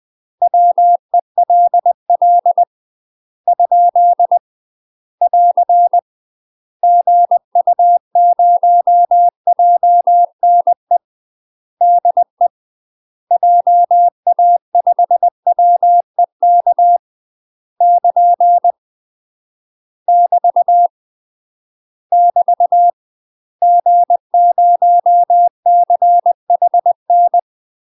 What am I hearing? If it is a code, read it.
Morse code, heard as a keyed tone, about 20 wpm, 700 Hz. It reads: WELL ? <AR> GU0JNE DE JA5WEK <KN> <BT> <BT> G0CHN
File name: UK MASTER QSOs TRAINING / 20WPM / UK MASTER QSOs 0060